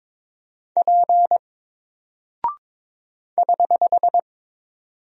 Morse code, heard as a keyed tone, about 22 wpm, 700 Hz. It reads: P <HH>
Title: Single Character (Letter:Number:Punctuation:Prosign) 22wpm